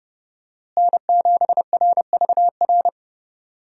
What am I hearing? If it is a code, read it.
Morse code, heard as a keyed tone, about 30 wpm, 700 Hz. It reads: N7RVR
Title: USA Call Signs - 5 Characters - Repeated 2x - 30wpm